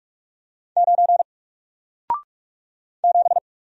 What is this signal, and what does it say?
Morse code, heard as a keyed tone, about 44 wpm, 700 Hz. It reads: 9 7
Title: Single Letter-Number Alt 45wpm